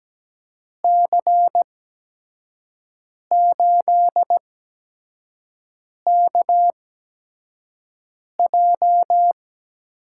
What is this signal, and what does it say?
Morse code, heard as a keyed tone, about 17 wpm, 700 Hz. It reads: C 8 K J
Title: USA Call Signs - 8x Character Spacing - 17wpm